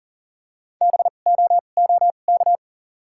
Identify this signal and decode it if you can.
Morse code, heard as a keyed tone, about 37 wpm, spaced as for 28 wpm, 700 Hz. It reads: BOOK
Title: 2x Character Spacing - Top 300 Words - 40wpm